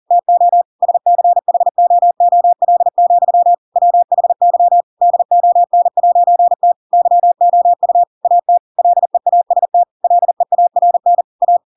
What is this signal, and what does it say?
Morse code, heard as a keyed tone, about 40 wpm, 700 Hz. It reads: TO SCHOOL, WHY DON'T YOU AT LEAST LEARN A
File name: Pinocchio / Pinocchio ch4 40 40 ik1hge